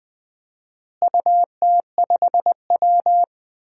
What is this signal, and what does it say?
Morse code, heard as a keyed tone, about 20 wpm, 700 Hz. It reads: UT5W
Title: DXCC-100 Call Signs - 4 Characters - Repeated 2x - 20wpm